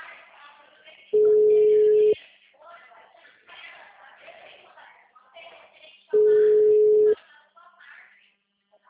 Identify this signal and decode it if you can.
Ringback tone (Europe, 425 Hz), with speech.
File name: out- -2002- - - .566